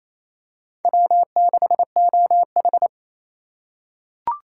Morse code, heard as a keyed tone, about 28 wpm, 700 Hz. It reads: W6OH
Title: USA Call Signs - 4 Characters - 3x Character Spacing - 28wpm